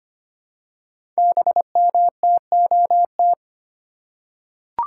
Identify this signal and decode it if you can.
Morse code, heard as a keyed tone, about 25 wpm, 700 Hz. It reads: BMTOT
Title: Code Groups - 5 Letters - 25wpm